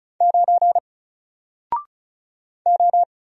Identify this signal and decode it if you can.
Morse code, heard as a keyed tone, about 35 wpm, 700 Hz. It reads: 9 O
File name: Single-Letter-Number-35wpm